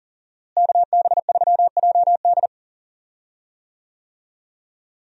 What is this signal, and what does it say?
Morse code, heard as a keyed tone, about 39 wpm, 700 Hz. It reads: KB3JD
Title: USA Call Signs - 5 Characters - Repeated 3x - 40wpm